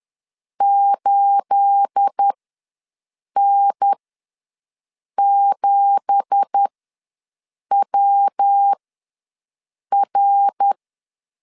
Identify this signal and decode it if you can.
Morse code, heard as a keyed tone, about 10 wpm, spaced as for 6 wpm, 800 Hz. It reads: 8N7WR
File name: LEZIONI / 10 SIGNS / 10C DOT ws 10wpm 4213